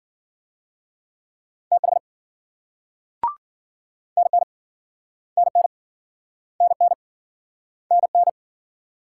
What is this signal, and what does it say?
Morse code, heard as a keyed tone, about 53 wpm, 700 Hz. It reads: TH NN NN NN NN